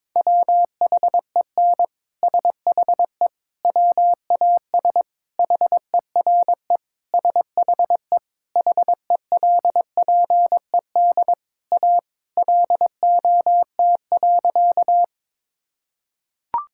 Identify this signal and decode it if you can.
Morse code, heard as a keyed tone, about 22 wpm, 700 Hz. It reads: WHEN SHE WAS HERE SHE HELPED A LOT.
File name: practice / 22wpm / Sentences with 3-Word Phrases - 22wpm